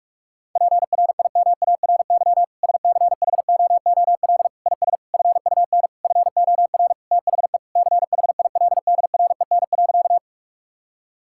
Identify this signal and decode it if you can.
Morse code, heard as a keyed tone, about 46 wpm, 700 Hz. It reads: PRIMARY SCHOOL IS FUN FOR THE CHILDREN.
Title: Sentences from Top 1000 Words - Review Part 5 - 45wpm